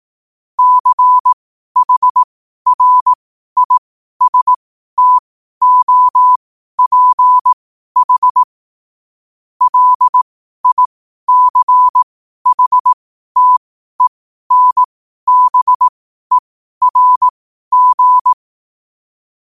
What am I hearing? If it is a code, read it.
Morse code, heard as a keyed tone, about 18 wpm, spaced as for 12 wpm, 1000 Hz. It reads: CHRISTOPH LICHTENBERG
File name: qotdfiles / 12wpm